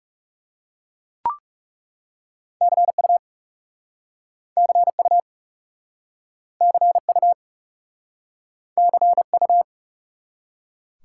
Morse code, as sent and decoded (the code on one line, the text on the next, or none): -.-. ..- / -.-. ..- / -.-. ..- / -.-. ..-
CU CU CU CU